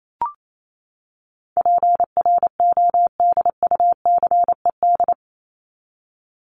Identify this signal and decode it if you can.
Morse code, heard as a keyed tone, about 28 wpm, 700 Hz. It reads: PRODUCED
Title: ED Suffix - Top 100 Words with Base Words 28wpm